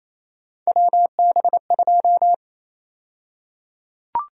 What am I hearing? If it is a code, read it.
Morse code, heard as a keyed tone, about 28 wpm, 700 Hz. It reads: WB2
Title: USA Call Sign Prefixes - Repeated 2x - 28wpm